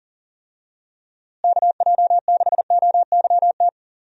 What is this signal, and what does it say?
Morse code, heard as a keyed tone, about 40 wpm, 700 Hz. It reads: KJ6OYT
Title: USA Call Signs - 6 Characters - Repeated 2x - 40wpm